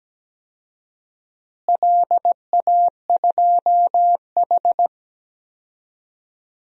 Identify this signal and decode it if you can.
Morse code, heard as a keyed tone, about 17 wpm, 700 Hz. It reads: LA2H
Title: DXCC-100 Call Signs - 4 Characters - Repeated 2x - 17wpm